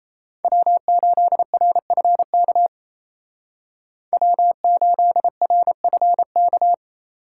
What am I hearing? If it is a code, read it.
Morse code, heard as a keyed tone, about 35 wpm, 700 Hz. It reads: W8RFK W8RFK